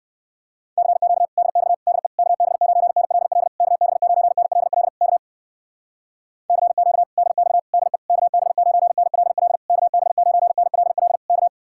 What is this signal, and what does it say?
Morse code, heard as a keyed tone, about 60 wpm, 700 Hz. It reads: CQ DX DE KB9NLL KB9NLL K CQ DX DE KB9NLL KB9NLL K